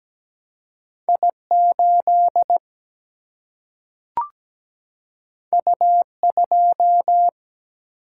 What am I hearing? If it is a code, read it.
Morse code, heard as a keyed tone, about 17 wpm, 700 Hz. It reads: I8 U2
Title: Single-Letter Single-Number 17wpm